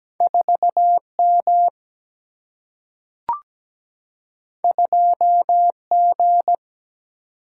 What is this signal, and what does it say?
Morse code, heard as a keyed tone, about 17 wpm, 700 Hz. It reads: 4 M 2 G
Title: Single-Letter Single-Number 17wpm